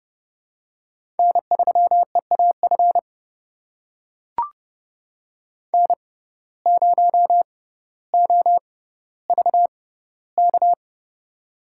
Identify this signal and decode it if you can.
Morse code, heard as a keyed tone, about 30 wpm, spaced as for 15 wpm, 700 Hz. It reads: N3EAF N0OVK
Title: USA Call Signs - 5 Characters - 6x Character Spacing - 30wpm